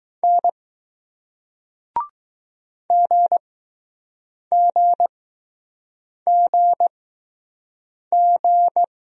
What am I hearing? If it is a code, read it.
Morse code, heard as a keyed tone, about 20 wpm, spaced as for 6 wpm, 700 Hz. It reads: N GGGG